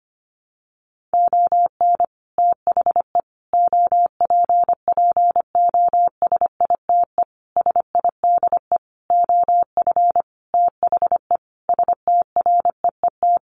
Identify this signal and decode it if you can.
Morse code, heard as a keyed tone, about 25 wpm, 700 Hz. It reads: ON THE OPPOSITE SIDE OF THE STREET
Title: Sentences from Top 1400 Words 25wpm